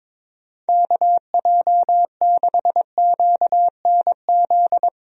Morse code, heard as a keyed tone, about 22 wpm, 700 Hz. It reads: KJ6QNZ
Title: USA Call Signs - 6 Characters - Repeated 2x - 22wpm